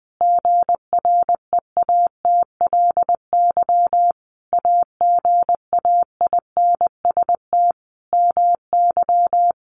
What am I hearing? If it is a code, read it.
Morse code, heard as a keyed tone, about 20 wpm, 700 Hz. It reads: GREATLY AGAINST MY
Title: Sentences from Top 1400 Words 20wpm